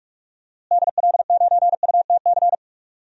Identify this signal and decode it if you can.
Morse code, heard as a keyed tone, about 46 wpm, 700 Hz. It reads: DL9UTC
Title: DXCC-100 Call Signs - 6 Characters - Repeated 3x - 45wpm